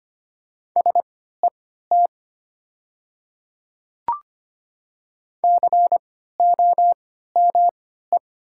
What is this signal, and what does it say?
Morse code, heard as a keyed tone, about 25 wpm, spaced as for 14 wpm, 700 Hz. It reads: SET COME
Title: 3x Character Spacing - Top 300 Words - 25wpm